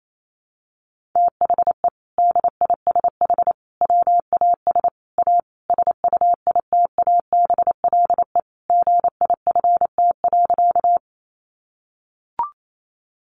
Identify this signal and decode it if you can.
Morse code, heard as a keyed tone, about 28 wpm, 700 Hz. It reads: THE DISH WAS A SUITABLE GIFT.
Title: Sentences from Top 1400 Words 28wpm